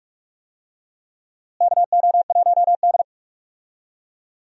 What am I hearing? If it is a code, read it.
Morse code, heard as a keyed tone, about 46 wpm, 700 Hz. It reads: KO1D
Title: USA Call Signs - 4 Characters - Repeated 3x - 45wpm